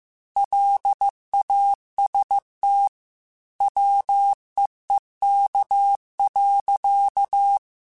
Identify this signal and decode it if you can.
Morse code, heard as a keyed tone, about 15 wpm, 800 Hz. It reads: LAST WEEK.
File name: QSOs / QSOs-mp3s / WD0BGP 2@13